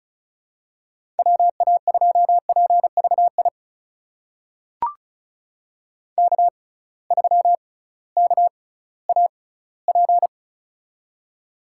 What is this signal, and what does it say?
Morse code, heard as a keyed tone, about 36 wpm, spaced as for 19 wpm, 700 Hz. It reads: WA2PVI K3KAP
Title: USA Call Signs - 6x Character Spacing - 35wpm